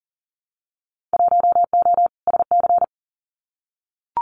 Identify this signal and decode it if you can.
Morse code, heard as a keyed tone, about 40 wpm, 700 Hz. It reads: 1O SC
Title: Winter Field Day Exchanges 40wpm